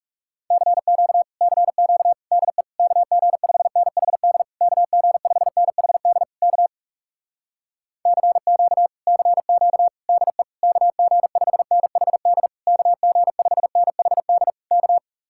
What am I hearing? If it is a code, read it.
Morse code, heard as a keyed tone, about 40 wpm, 700 Hz. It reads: CQ CQ DE KG5NHD KG5NHD K CQ CQ DE KG5NHD KG5NHD K